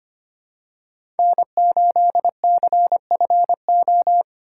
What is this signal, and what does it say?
Morse code, heard as a keyed tone, about 25 wpm, 700 Hz. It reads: N8CFO
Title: USA Call Signs - 5 Characters - 2x Character Spacing - 25wpm